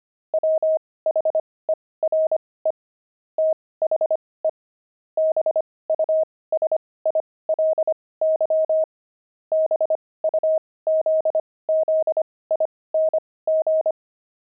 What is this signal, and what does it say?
Morse code, heard as a keyed tone, about 25 wpm, spaced as for 18 wpm, 630 Hz. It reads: WHERE THE BUSILY BUZZING
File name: pangram-fs-18-cs-25-beehive